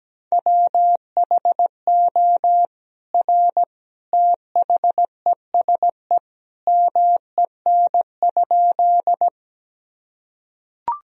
Morse code, heard as a keyed tone, about 17 wpm, 700 Hz. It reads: WHO R THESE MEN?
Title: Sentences from Top 200 Words and Abbreviations 17wpm